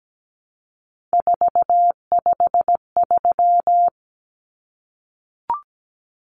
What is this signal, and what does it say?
Morse code, heard as a keyed tone, about 17 wpm, 700 Hz. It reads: 453
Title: Common Cut Numbers - 3-Digit 17wpm